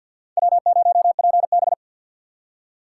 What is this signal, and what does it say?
Morse code, heard as a keyed tone, about 52 wpm, 700 Hz. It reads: W0PB
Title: USA Call Signs - 4 Characters - 2x Character Spacing - 50wpm